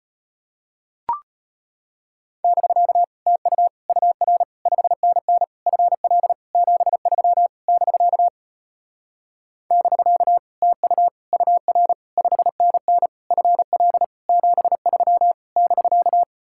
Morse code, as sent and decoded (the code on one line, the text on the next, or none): -...-.- / - ..- / ..- .-. / ..... -. -. / ..-. .-.. / --... ...-- / -...-.- / -...-.- / - ..- / ..- .-. / ..... -. -. / ..-. .-.. / --... ...-- / -...-.-
<BK> TU UR 5NN FL 73 <BK> <BK> TU UR 5NN FL 73 <BK>